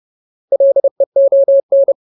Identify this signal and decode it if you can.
Morse code, heard as a keyed tone, about 30 wpm, 550 Hz. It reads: LEON
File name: Ultimate Word List / 30WPM Ultimate